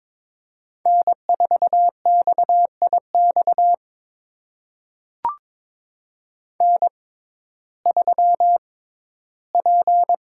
Morse code, heard as a keyed tone, about 22 wpm, spaced as for 14 wpm, 700 Hz. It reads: N4XIX N3P
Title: USA Call Signs - 5 Characters - 6x Character Spacing - 22wpm